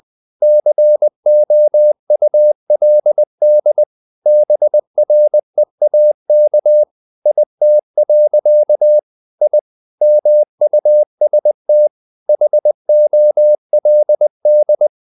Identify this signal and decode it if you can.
Morse code, heard as a keyed tone, about 20 wpm, 600 Hz. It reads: COULD BREAK IT. I MUST HOLD
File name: the old man and the sea / the old man and the sea part 1 20wpm